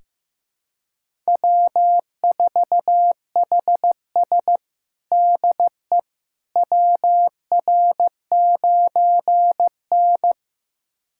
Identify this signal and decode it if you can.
Morse code, heard as a keyed tone, about 15 wpm, 700 Hz. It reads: W4HS DE WR9N
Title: USA Callsign DE Callsign - 4 Characters - 15wpm